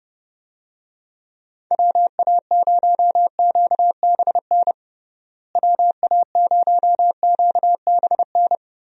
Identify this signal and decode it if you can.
Morse code, heard as a keyed tone, about 30 wpm, 700 Hz. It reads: WA0QBN WA0QBN
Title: USA Call Signs - 6 Characters - Repeated 3x - 30wpm